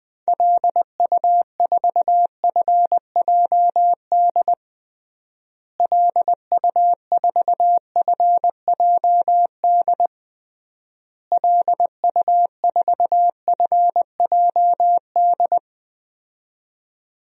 Morse code, heard as a keyed tone, about 20 wpm, 700 Hz. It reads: LU4FJD LU4FJD LU4FJD
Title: DXCC-100 Call Signs - 6 Characters - Repeated 3x - 20wpm